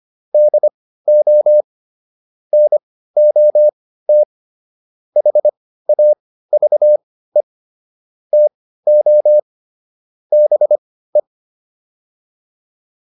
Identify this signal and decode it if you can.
Morse code, heard as a keyed tone, about 25 wpm, spaced as for 15 wpm, 600 Hz. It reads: DO NOT HAVE TO BE
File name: WD405 15